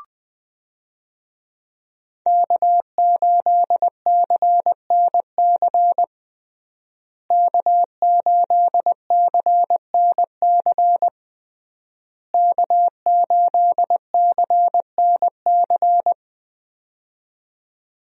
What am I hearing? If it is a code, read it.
Morse code, heard as a keyed tone, about 20 wpm, 700 Hz. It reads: K8CNC K8CNC K8CNC
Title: USA Call Signs - 5 Characters - Repeated 3x - 20wpm